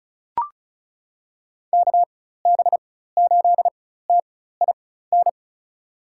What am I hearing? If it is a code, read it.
Morse code, heard as a keyed tone, about 34 wpm, spaced as for 16 wpm, 700 Hz. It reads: KB8TIN
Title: USA Call Signs - 6 Characters - 4x Character Spacing - 35wpm